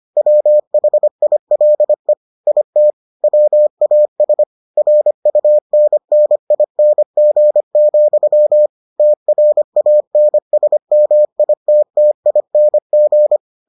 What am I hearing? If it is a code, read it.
Morse code, heard as a keyed tone, about 25 wpm, 600 Hz. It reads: WHILE IT WAS RUNNING, TRANSMITTING